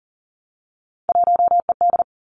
Morse code, heard as a keyed tone, about 39 wpm, 700 Hz. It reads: 1ED
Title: Three Challenging Letter-Number Combinations 40wpm